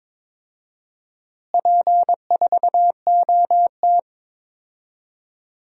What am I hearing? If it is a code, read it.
Morse code, heard as a keyed tone, about 22 wpm, 700 Hz. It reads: P4OT
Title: DXCC-100 Call Signs - 4 Characters - Repeated 2x - 22wpm